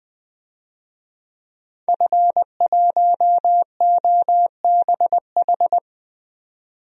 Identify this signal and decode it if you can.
Morse code, heard as a keyed tone, about 21 wpm, 700 Hz. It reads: F1OBH
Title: DXCC-100 Call Signs - 5 Characters - 20wpm